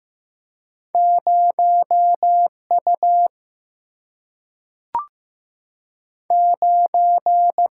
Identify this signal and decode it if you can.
Morse code, heard as a keyed tone, about 15 wpm, 700 Hz. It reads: 0 U 9
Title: Single-Letter Single-Number 15wpm